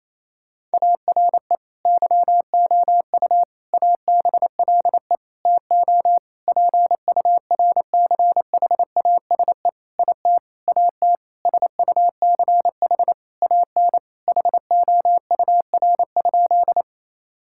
Morse code, heard as a keyed tone, about 28 wpm, 700 Hz. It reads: ARE YOU ABLE TO PURCHASE IT AT SUCH AN HOUR?
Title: Sentences from Top 1000 Words - Review Part 5 - 28wpm